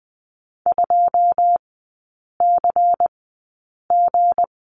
Morse code, heard as a keyed tone, about 20 wpm, spaced as for 8 wpm, 700 Hz. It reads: 2CG
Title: 178 Lesson 29e - Slash - Three Characters 20wpm